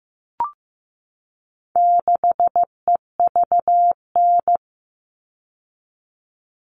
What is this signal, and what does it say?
Morse code, heard as a keyed tone, about 15 wpm, 700 Hz. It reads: 6EVN
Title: All Cut Numbers - 4-Digit 15wpm